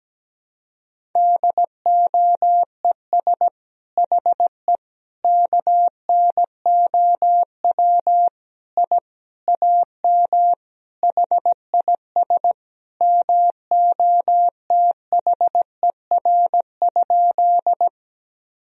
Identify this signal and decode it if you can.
Morse code, heard as a keyed tone, about 17 wpm, 700 Hz. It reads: DOES HE KNOW I AM HIS MOTHER?
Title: Sentences from Top 1000 Words - Review Part 2 - 17wpm